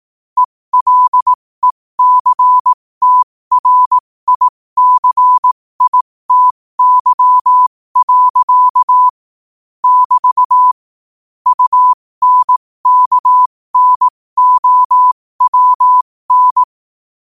Morse code, heard as a keyed tone, about 18 wpm, spaced as for 15 wpm, 1000 Hz. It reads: ELECTRICITY. <BT> UNKNOWN
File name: qotdfiles / 15wpm